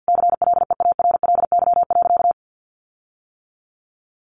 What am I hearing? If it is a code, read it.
Morse code, heard as a keyed tone, about 50 wpm, 700 Hz. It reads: CLEARLY.
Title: Sentences from Top 1300 Words 50wpm